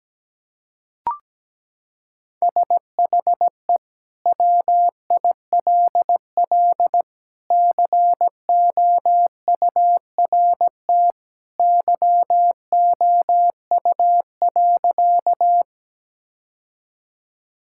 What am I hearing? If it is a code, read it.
Morse code, heard as a keyed tone, about 17 wpm, 700 Hz. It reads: SHE WILL COURT YOU.
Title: Sentences from Top 1600 Words 17wpm